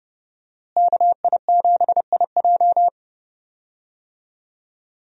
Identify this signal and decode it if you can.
Morse code, heard as a keyed tone, about 30 wpm, 700 Hz. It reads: KI7IJ
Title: USA Call Signs - 5 Characters - Repeated 3x - 30wpm